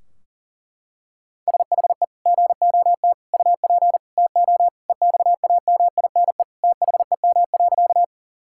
Morse code, heard as a keyed tone, about 40 wpm, 700 Hz. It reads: SHE GOT UP TO EXAMINE THEM.
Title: Sentences from Top 1600 Words 40wpm